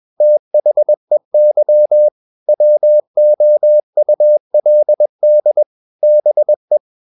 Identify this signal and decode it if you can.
Morse code, heard as a keyed tone, about 21 wpm, 600 Hz. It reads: THEY WOULD BE
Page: CW med Gnister